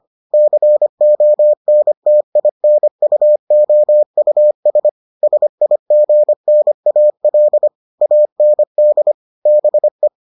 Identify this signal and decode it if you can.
Morse code, heard as a keyed tone, about 25 wpm, 600 Hz. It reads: CONTINUOUS SIGNAL AND BE